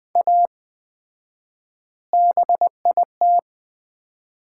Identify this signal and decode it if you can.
Morse code, heard as a keyed tone, about 21 wpm, 700 Hz. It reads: A BIT
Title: 4x Word-Spacing - Sets of 4 Words - 20wpm